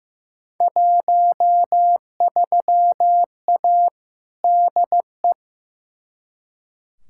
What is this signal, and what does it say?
Morse code, heard as a keyed tone, about 16 wpm, 700 Hz. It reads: 13A DE
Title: Field Day Exchanges Encore - 15wpm